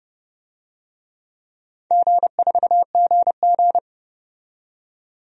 Morse code, heard as a keyed tone, about 30 wpm, 700 Hz. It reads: G4GG
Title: DXCC-100 Call Signs - 4 Characters - 30wpm